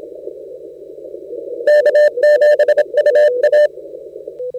Morse code, heard as a keyed tone, about 27 wpm, 600 Hz. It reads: K7UA
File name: Audio.CQWW.CW.2013.from.KB7Q / 10minute snippet PJ2T 40M KB7Q op